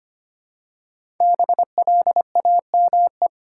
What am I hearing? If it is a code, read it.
Morse code, heard as a keyed tone, about 25 wpm, 700 Hz. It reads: BLAME
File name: 5 Letter Words - First 3 Letters Given - 25wpm